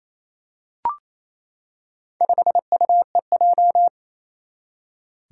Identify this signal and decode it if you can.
Morse code, heard as a keyed tone, about 28 wpm, 700 Hz. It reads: 5UEJ
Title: Four Challenging Letter-Number Combinations 28wpm